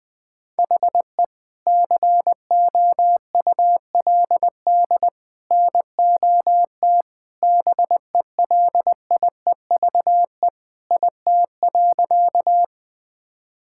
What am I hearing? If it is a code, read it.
Morse code, heard as a keyed tone, about 21 wpm, 700 Hz. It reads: HE COULD NOT BELIEVE IT.
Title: Sentences with 3-Word Phrases - 20wpm